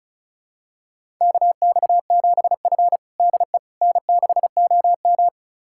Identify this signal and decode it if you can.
Morse code, heard as a keyed tone, about 36 wpm, 700 Hz. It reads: KX7F DE N6OM
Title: USA Callsign DE Callsign - 4 Characters - 35wpm